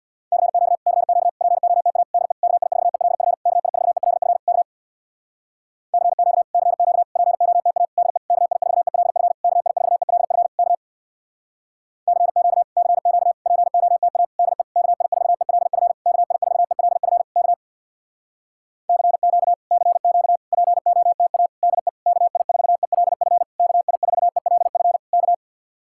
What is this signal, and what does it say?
Morse code, heard as a keyed tone, about 66 wpm, 700 Hz. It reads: CQ CQ POTA DE KI4ELF KI4ELF K CQ CQ POTA DE KI4ELF KI4ELF K CQ CQ POTA DE KI4ELF KI4ELF K CQ CQ POTA DE KI4ELF KI4ELF K